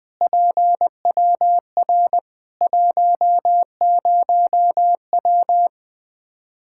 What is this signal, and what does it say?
Morse code, heard as a keyed tone, about 20 wpm, 700 Hz. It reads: PWR 10W
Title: Practice QSO - Exchange 2 - Info - 20wpm